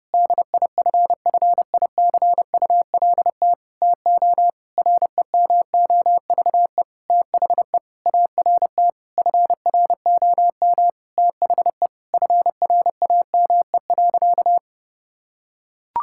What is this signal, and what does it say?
Morse code, heard as a keyed tone, about 31 wpm, 700 Hz. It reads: DIFFICULT TO REMOVE THE ART FROM THE FRAME.
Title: Sentences from Top 1000 Words - Review Part 6 - 30wpm